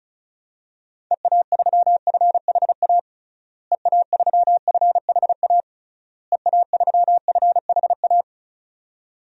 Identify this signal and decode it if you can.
Morse code, heard as a keyed tone, about 36 wpm, 700 Hz. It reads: EA3FHA EA3FHA EA3FHA
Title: DXCC-100 Call Signs - 6 Characters - Repeated 3x - 35wpm